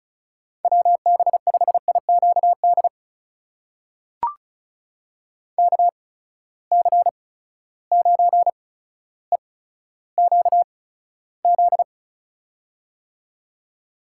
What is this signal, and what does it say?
Morse code, heard as a keyed tone, about 36 wpm, spaced as for 15 wpm, 700 Hz. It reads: WB5IQD KC9EQZ
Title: USA Call Signs - 6 Characters - 8x Character Spacing - 35wpm